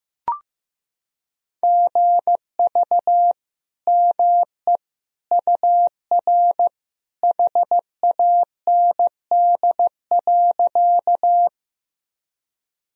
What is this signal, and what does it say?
Morse code, heard as a keyed tone, about 15 wpm, 700 Hz. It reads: GV ME UR HAND.
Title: Sentences from Top 200 Words and Abbreviations 15wpm